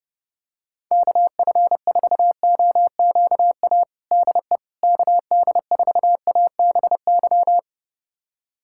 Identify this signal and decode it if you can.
Morse code, heard as a keyed tone, about 32 wpm, 700 Hz. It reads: KF4OQA DE KD4ABY
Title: USA Callsign DE Callsign 30wpm